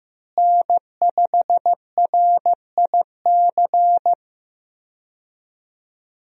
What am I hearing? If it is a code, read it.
Morse code, heard as a keyed tone, about 15 wpm, 700 Hz. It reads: N5RIC
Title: USA Call Signs - 5 Characters - Repeated 3x - 15wpm